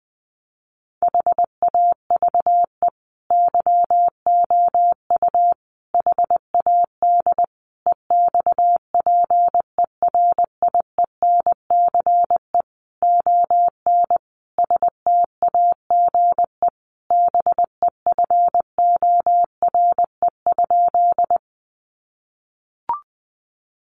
Morse code, heard as a keyed tone, about 20 wpm, 700 Hz. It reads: HAVE YOU HAD EXPERIENCE ON STAGE BEFORE?
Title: Sentences from Top 1200 Words 20wpm